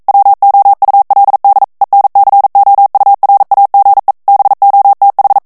Morse code, heard as a keyed tone, about 42 wpm, 800 Hz. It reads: WOULD ENCOURAGE BOTH